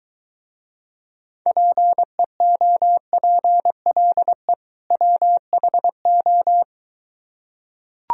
Morse code, heard as a keyed tone, about 23 wpm, 700 Hz. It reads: PEOPLE WHO